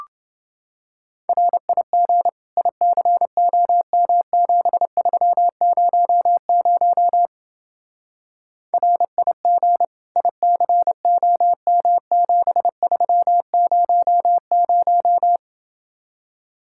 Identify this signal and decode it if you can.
Morse code, heard as a keyed tone, about 28 wpm, 700 Hz. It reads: RIG ICOM7300 RIG ICOM7300